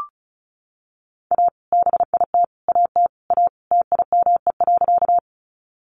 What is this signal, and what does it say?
Morse code, heard as a keyed tone, about 34 wpm, 700 Hz. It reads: A BIT AT A TIME.
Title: Sentences from Top 900 Words 35wpm